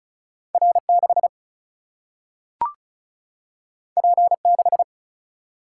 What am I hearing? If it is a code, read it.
Morse code, heard as a keyed tone, about 36 wpm, 700 Hz. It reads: R6 P6
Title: Single-Letter Single-Number 35wpm